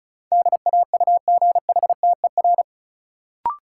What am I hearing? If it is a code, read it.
Morse code, heard as a keyed tone, about 36 wpm, 700 Hz. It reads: DAUGHTER
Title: Sentences from Top 1500 Words - New Word Drill Alt - 35wpm